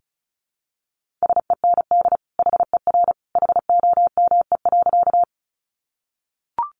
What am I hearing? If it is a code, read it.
Morse code, heard as a keyed tone, about 34 wpm, 700 Hz. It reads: SEND HER HOME.
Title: Sentences from Top 1500 Words 35wpm